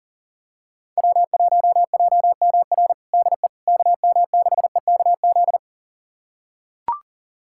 Morse code, heard as a keyed tone, about 40 wpm, 700 Hz. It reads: W1JMR DE KM6EKZ
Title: USA Callsign DE Callsign 40wpm